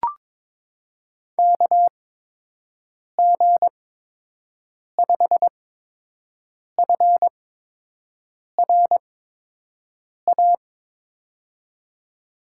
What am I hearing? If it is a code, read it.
Morse code, heard as a keyed tone, about 22 wpm, spaced as for 6 wpm, 700 Hz. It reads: KG5FRA
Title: USA Call Signs - 6 Characters - 8x Character Spacing - 22wpm